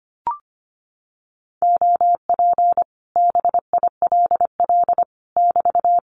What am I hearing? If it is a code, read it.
Morse code, heard as a keyed tone, about 25 wpm, 700 Hz. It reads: OP BILL <BT>
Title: Common QSO Phrases Encore 25wpm